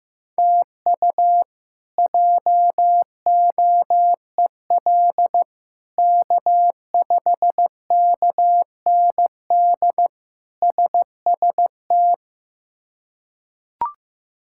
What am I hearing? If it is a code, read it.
Morse code, heard as a keyed tone, about 16 wpm, 700 Hz. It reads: TU JOEL K5KND SST
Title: K1USN SST QSO Practice - 15wpm